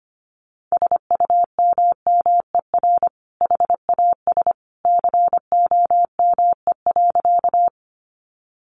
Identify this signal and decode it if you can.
Morse code, heard as a keyed tone, about 25 wpm, 700 Hz. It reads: SUMMER HAS COME.
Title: Sentences from Top 1100 Words 25wpm